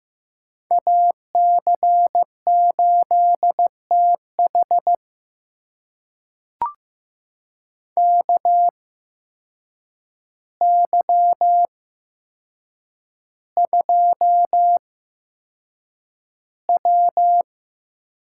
Morse code, heard as a keyed tone, about 15 wpm, spaced as for 7 wpm, 700 Hz. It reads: AC8TH KY2W
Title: USA Call Signs - 8x Character Spacing - 15wpm